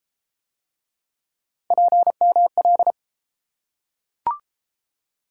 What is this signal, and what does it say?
Morse code, heard as a keyed tone, about 31 wpm, 700 Hz. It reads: PML